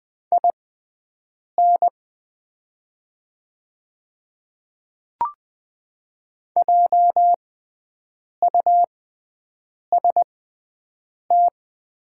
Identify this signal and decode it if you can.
Morse code, heard as a keyed tone, about 20 wpm, spaced as for 7 wpm, 690 Hz. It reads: IN JUST
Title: 6x Character Spacing - Top 200 Words - 20wpm